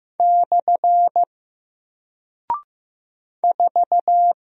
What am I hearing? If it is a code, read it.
Morse code, heard as a keyed tone, about 15 wpm, 700 Hz. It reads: / 4
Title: Single Character (Letter:Number:Punctuation:Prosign) Alt 15wpm